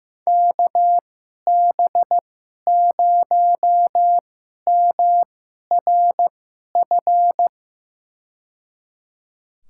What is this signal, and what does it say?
Morse code, heard as a keyed tone, about 15 wpm, spaced as for 11 wpm, 700 Hz. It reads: KB0MRF
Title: USA Call Signs - 2x Character Spacing - 15wpm